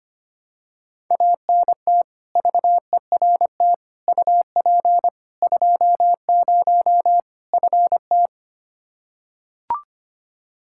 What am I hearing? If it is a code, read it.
Morse code, heard as a keyed tone, about 25 wpm, 700 Hz. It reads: ANT VERT UP 20 FT
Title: K1USN SST QSO Practice 25wpm